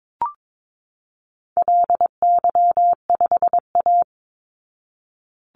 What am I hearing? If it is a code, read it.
Morse code, heard as a keyed tone, about 22 wpm, 700 Hz. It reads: LY5A
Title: Call Signs Repeated 2x - 22wpm